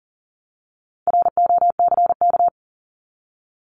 Morse code, heard as a keyed tone, about 40 wpm, 700 Hz. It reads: ROCK
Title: Top 500 Words Repeated 40wpm